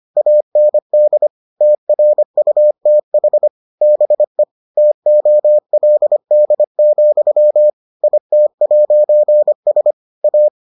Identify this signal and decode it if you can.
Morse code, heard as a keyed tone, about 25 wpm, 600 Hz. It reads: AND TRUTH BE TOLD, IT'S A